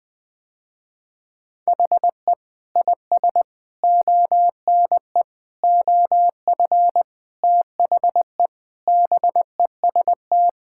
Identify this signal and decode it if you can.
Morse code, heard as a keyed tone, about 20 wpm, 700 Hz. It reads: HE IS ONE OF THE BEST
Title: Sentences with 7-Word Phrases - Spoken Beginning - 20wpm